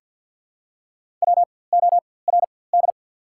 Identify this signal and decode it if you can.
Morse code, heard as a keyed tone, about 54 wpm, spaced as for 24 wpm, 690 Hz. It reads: WORD
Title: 4x Character Spacing - Top 100 Words - 50wpm